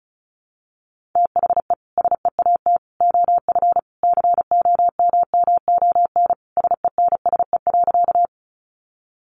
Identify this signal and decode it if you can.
Morse code, heard as a keyed tone, about 34 wpm, 700 Hz. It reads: THE SEAT OF COMMON SENSE.
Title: Sentences from Top 1500 Words 35wpm